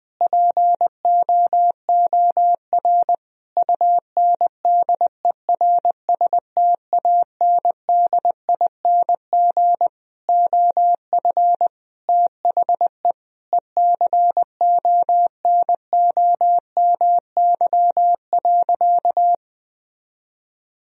Morse code, heard as a keyed tone, about 20 wpm, 700 Hz. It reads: POOR UNDERSTANDING OF THE ECONOMY.
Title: Sentences from Top 1000 Words - Review Part 3 - 20wpm